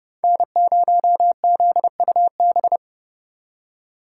Morse code, heard as a keyed tone, about 28 wpm, 700 Hz. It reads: N0ZUB